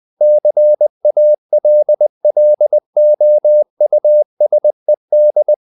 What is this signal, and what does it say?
Morse code, heard as a keyed tone, about 20 wpm, 600 Hz. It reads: CALLOUSED
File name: the old man and the sea part 2 20wpm